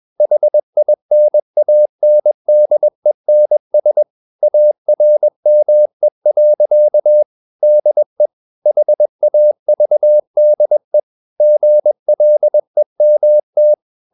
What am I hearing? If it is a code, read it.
Morse code, heard as a keyed tone, about 21 wpm, 600 Hz. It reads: HINANDENS ARME. DE HAVDE GLEMT
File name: Ferie QRS 21wpm